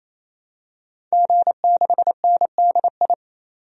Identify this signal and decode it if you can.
Morse code, heard as a keyed tone, about 28 wpm, 700 Hz. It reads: G6NDI
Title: DXCC-100 Call Signs - 5 Characters - 28wpm